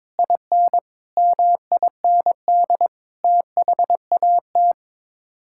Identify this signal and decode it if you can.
Morse code, heard as a keyed tone, about 22 wpm, 700 Hz. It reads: IN MIND THAT
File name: practice / 22wpm / Sets of 8 Words - 22wpm